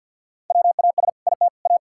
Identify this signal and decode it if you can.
Morse code, heard as a keyed tone, about 52 wpm, 700 Hz. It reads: WAS IT A
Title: Sentences from Top 1000 Words - Review Part 5 - 50wpm